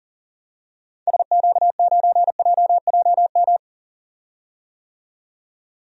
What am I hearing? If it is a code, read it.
Morse code, heard as a keyed tone, about 40 wpm, 700 Hz. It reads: SQ9JJM
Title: DXCC-100 Call Signs - 6 Characters - 40wpm